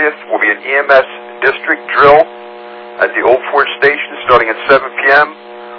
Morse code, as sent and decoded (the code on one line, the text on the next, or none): none